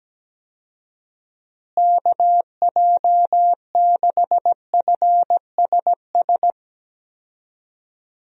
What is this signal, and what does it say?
Morse code, heard as a keyed tone, about 17 wpm, 700 Hz. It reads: KJ6FSS
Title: USA Call Signs - 6 Characters - Repeated 2x - 17wpm